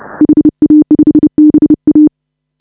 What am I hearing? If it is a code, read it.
Morse code, heard as a keyed tone, about 31 wpm, 310 Hz. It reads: HA5BA